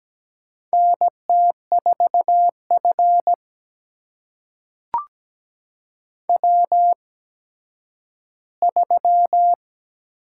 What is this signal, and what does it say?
Morse code, heard as a keyed tone, about 17 wpm, spaced as for 10 wpm, 700 Hz. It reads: NT4F W3
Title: USA Call Signs - 4 Characters - 8x Character Spacing - 17wpm